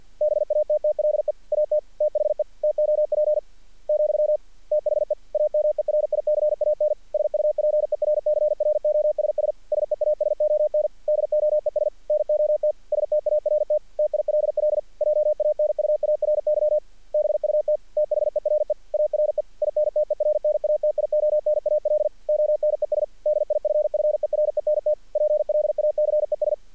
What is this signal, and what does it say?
Morse code, heard as a keyed tone, about 48 wpm, 600 Hz. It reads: BATTLE AT THE TOP , THE AMERICAN SUPERCROSS SEASON DOES NOT START TILL JANUARY BUT THERE ARE INTERNATIONAL ONES DIFFERENT PLACES
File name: archived / CRSnet